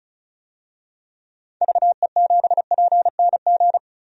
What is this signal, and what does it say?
Morse code, heard as a keyed tone, about 36 wpm, 700 Hz. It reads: VE7PNG
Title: DXCC-100 Call Signs - 6 Characters - 35wpm